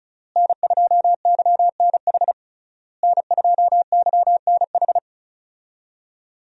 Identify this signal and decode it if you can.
Morse code, heard as a keyed tone, about 36 wpm, 700 Hz. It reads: N2YNH N2YNH
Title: USA Call Signs - 5 Characters - Repeated 3x - 35wpm